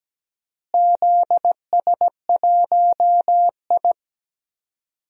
Morse code, heard as a keyed tone, about 17 wpm, 700 Hz. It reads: ZS1I
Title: DXCC-100 Call Signs - 4 Characters - Repeated 2x - 17wpm